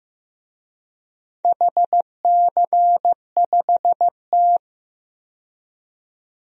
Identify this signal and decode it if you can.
Morse code, heard as a keyed tone, about 15 wpm, 700 Hz. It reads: HC5T
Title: DXCC-100 Call Signs - 4 Characters - 15wpm